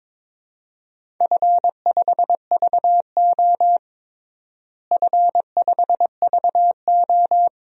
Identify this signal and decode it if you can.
Morse code, heard as a keyed tone, about 22 wpm, 700 Hz. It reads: F5VO F5VO
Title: DXCC-100 Call Signs - 4 Characters - Repeated 3x - 22wpm